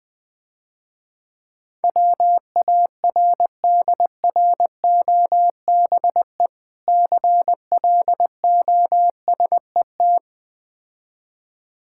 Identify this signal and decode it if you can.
Morse code, heard as a keyed tone, about 20 wpm, 700 Hz. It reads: WARDROBE CLOSET
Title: Difficult Words - Speed-Racing with Hints - 20wpm